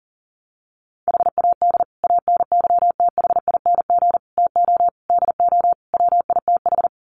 Morse code, heard as a keyed tone, about 40 wpm, 700 Hz. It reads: HAD ANYTHING TO DO WITH
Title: Sets of 5 Words Encore 40wpm